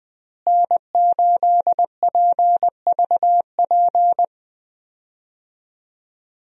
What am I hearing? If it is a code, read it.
Morse code, heard as a keyed tone, about 20 wpm, 700 Hz. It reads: N8PVP
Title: USA Call Signs - 5 Characters - Repeated 3x - 20wpm